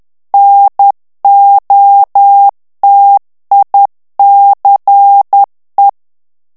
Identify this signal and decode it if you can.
Morse code, heard as a keyed tone, about 10 wpm, 800 Hz. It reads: NOTICE
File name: ANGLAIS texte clair 3 vitesse 10 mots minute